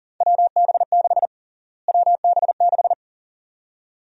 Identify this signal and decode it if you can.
Morse code, heard as a keyed tone, about 40 wpm, 700 Hz. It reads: WB6 WB6
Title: USA Call Sign Prefixes - Repeated 2x - 40wpm